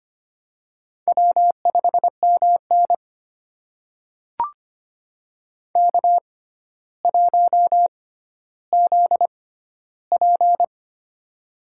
Morse code, heard as a keyed tone, about 25 wpm, spaced as for 13 wpm, 700 Hz. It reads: W5MN K1ZP
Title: USA Call Signs - 4 Characters - 6x Character Spacing - 25wpm